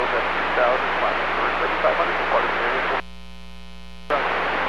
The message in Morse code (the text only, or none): none